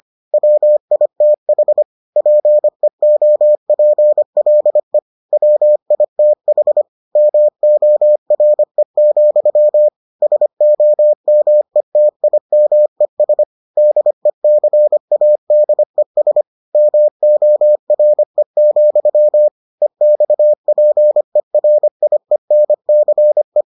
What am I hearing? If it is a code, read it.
Morse code, heard as a keyed tone, about 25 wpm, 600 Hz. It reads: WITH PEOPLE WITH MORE, SOMETIMES DECADES MORE, EXPERIENCE